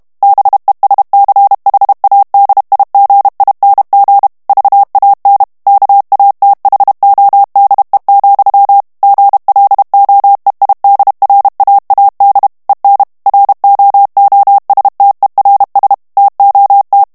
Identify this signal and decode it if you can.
Morse code, heard as a keyed tone, about 32 wpm, 800 Hz. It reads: BESCHADIGING VAN KATHODE, GLOEIDRAAD EN ROOSTERS TOT
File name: morse mp3 / HOLLANDAIS texte clair 2 vitesse 30 mots minute